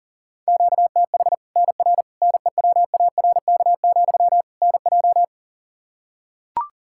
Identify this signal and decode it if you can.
Morse code, heard as a keyed tone, about 40 wpm, 700 Hz. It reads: QTH NR NEWARK, NJ
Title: Practice QSO - Exchange 1 - Beginning - 40wpm